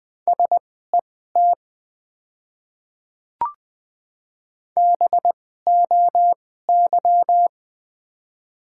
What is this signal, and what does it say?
Morse code, heard as a keyed tone, about 20 wpm, spaced as for 14 wpm, 700 Hz. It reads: SET BOY
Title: 2x Character Spacing - Top 200 Words - 20wpm